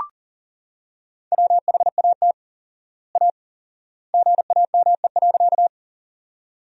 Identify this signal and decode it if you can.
Morse code, heard as a keyed tone, about 40 wpm, 700 Hz. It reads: WHAT A GAME.
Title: 4x Word-Spacing - Sentences from Top 500 Words - 40wpm